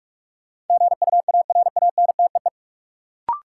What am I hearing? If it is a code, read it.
Morse code, heard as a keyed tone, about 42 wpm, 700 Hz. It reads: GUARANTEE
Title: Sentences from Top 1600 Words - New Word Drill - 45wpm